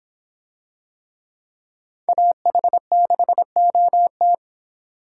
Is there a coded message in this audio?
Yes: Morse code from 2.1 to 4.3 s, heard as a keyed tone, about 26 wpm, 700 Hz. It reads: AH6OT